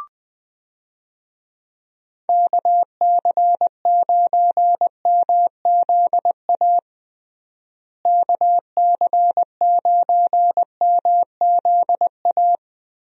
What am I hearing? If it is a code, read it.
Morse code, heard as a keyed tone, about 20 wpm, 700 Hz. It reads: KC9MZA KC9MZA
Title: USA Call Signs - 6 Characters - Repeated 3x - 20wpm